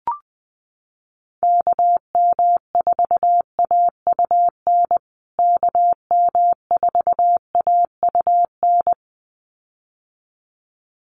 Morse code, heard as a keyed tone, about 20 wpm, 700 Hz. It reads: KM4AUN KM4AUN
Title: USA Call Signs Repeated 2x 20wpm